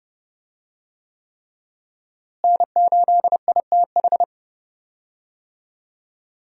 Morse code, heard as a keyed tone, about 31 wpm, 700 Hz. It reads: N8ITH
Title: USA Call Signs - 5 Characters - Repeated 3x - 30wpm